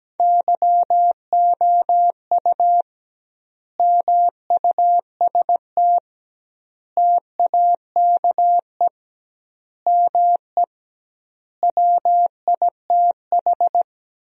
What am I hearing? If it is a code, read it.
Morse code, heard as a keyed tone, about 17 wpm, 700 Hz. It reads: YOU MUST TAKE ME WITH
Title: 2x Word-Spacing - Sentences from Top 200 Words - 17wpm